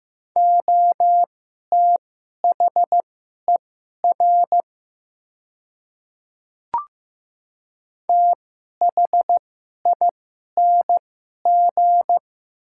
Morse code, heard as a keyed tone, about 15 wpm, spaced as for 11 wpm, 700 Hz. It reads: OTHER THING
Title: 2x Character Spacing - Top 300 Words - 15wpm